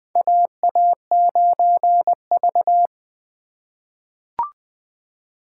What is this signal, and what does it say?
Morse code, heard as a keyed tone, about 20 wpm, 700 Hz. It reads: AA9V
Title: USA Call Signs - 4 Characters - 4x Character Spacing - 20wpm